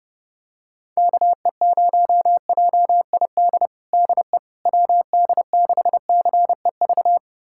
Morse code, heard as a keyed tone, about 30 wpm, 700 Hz. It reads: KE0JID DE WD6CEV
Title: USA Callsign DE Callsign - 6 Characters - 30wpm